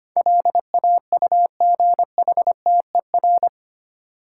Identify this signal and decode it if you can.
Morse code, heard as a keyed tone, about 25 wpm, 700 Hz. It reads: LAUGHTER
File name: Eight Letter Words - 25wpm